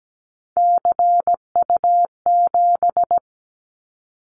Morse code, heard as a keyed tone, about 17 wpm, 700 Hz. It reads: CU7
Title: Callsign Prefixes for Top 50 DXCC - 17wpm